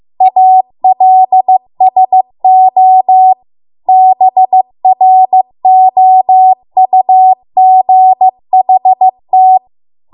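Morse code, heard as a keyed tone, about 15 wpm, 750 Hz. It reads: ALSO BROUGHT
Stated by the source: Morse code practice at 15 wpm from the national amateur radio association's code archive.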